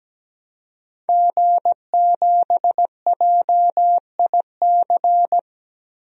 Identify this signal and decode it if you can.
Morse code, heard as a keyed tone, about 17 wpm, 700 Hz. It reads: G7JIC
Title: DXCC-100 Call Signs - 5 Characters - Repeated 2x - 17wpm